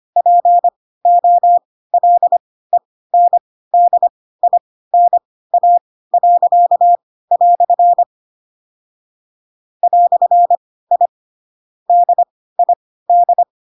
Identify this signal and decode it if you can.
Morse code, heard as a keyed tone, about 25 wpm, spaced as for 16 wpm, 700 Hz. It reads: POLENDINA." "I DID
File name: Pinocchio / Pinocchio ch2 10 25 ik1hge